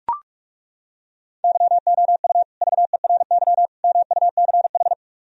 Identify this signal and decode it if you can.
Morse code, heard as a keyed tone, about 36 wpm, 700 Hz. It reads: YOU VERY MUCH
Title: Speed-Racing - Sets of 3 Words 30wpm